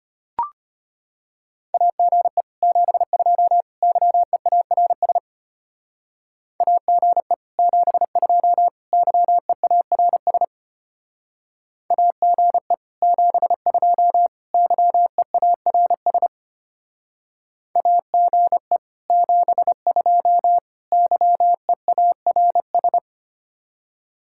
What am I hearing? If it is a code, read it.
Morse code, heard as a keyed tone, about 32 wpm, 700 Hz. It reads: AGE 72 YEARS AGE 72 YEARS AGE 72 YEARS AGE 72 YEARS